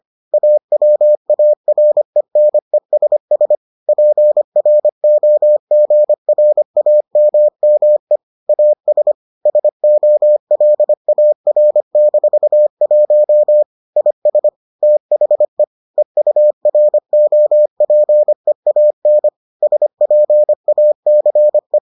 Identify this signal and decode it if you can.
Morse code, heard as a keyed tone, about 25 wpm, 600 Hz. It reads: AWARENESS PROGRAMME AS SOLAR-1 IS THE EUROPEAN SPACE